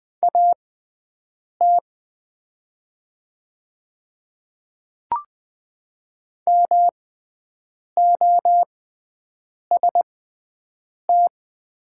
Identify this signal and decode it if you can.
Morse code, heard as a keyed tone, about 21 wpm, spaced as for 7 wpm, 700 Hz. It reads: AT MOST
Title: 6x Character Spacing - Top 300 Words - 20wpm